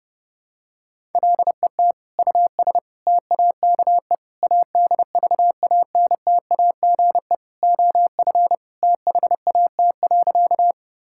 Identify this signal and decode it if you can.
Morse code, heard as a keyed tone, about 30 wpm, 700 Hz. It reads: LET US TAKE ADVANTAGE OF THAT.
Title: Sentences from Top 1000 Words - Review Part 6 - 30wpm